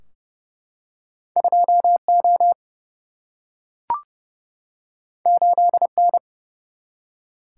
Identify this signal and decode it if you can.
Morse code, heard as a keyed tone, about 28 wpm, 700 Hz. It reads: 2 O 8 N
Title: Single-Letter Single-Number 30wpm